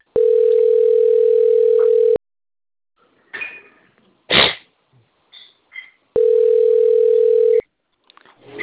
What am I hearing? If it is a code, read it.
Ringback tone (North America), 440 + 480 Hz.
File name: out- - - - - .62006